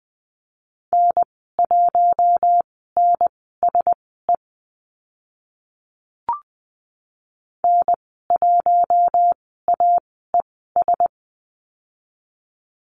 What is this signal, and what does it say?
Morse code, heard as a keyed tone, about 20 wpm, spaced as for 15 wpm, 700 Hz. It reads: N1NSE N1AES
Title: 042 Lesson 8g - Number 1 - Call Signs 20wpm